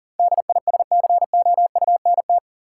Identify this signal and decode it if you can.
Morse code, heard as a keyed tone, about 40 wpm, 700 Hz. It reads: DISCOUNT
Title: Sentences from Top 1200 Words - New Word Drill - 40wpm